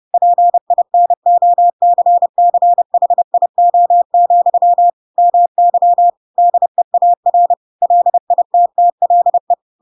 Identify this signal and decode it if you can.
Morse code, heard as a keyed tone, about 30 wpm, 700 Hz. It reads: PINOCCHIO, MY DEAR LITTLE
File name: Pinocchio / Pinocchio ch7 30 30 ik1hge